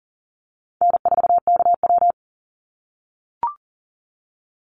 Morse code, heard as a keyed tone, about 40 wpm, 700 Hz. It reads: N4KW
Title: Easy Call Signs - 40wpm